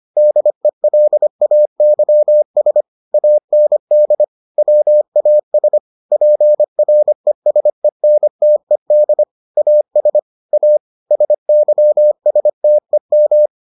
Morse code, heard as a keyed tone, about 25 wpm, 600 Hz. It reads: DELAYS AND WAS PRESENTED AS A SYSTEM